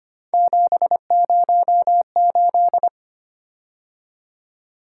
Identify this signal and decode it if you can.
Morse code, heard as a keyed tone, about 25 wpm, 700 Hz. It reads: 708
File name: Three Letter-Number Combinations - 25wpm